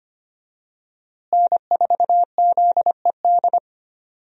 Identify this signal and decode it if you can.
Morse code, heard as a keyed tone, about 25 wpm, 700 Hz. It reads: N4ZED
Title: USA Call Signs - 5 Characters - Repeated 3x - 25wpm